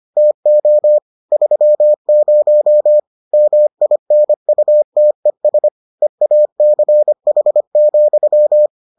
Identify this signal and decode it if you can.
Morse code, heard as a keyed tone, about 25 wpm, 600 Hz. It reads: TO 30 MINUTES EACH,